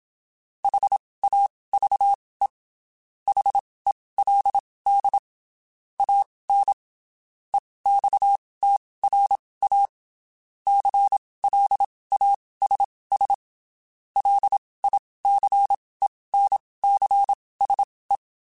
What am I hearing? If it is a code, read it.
Morse code, heard as a keyed tone, about 27 wpm, spaced as for 19 wpm, 800 Hz. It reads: HAVE HELD AN EXTRA CLASS LICENCSE
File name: (spaced) / QSOs Spaced-mp3s / kc7sac @25 sp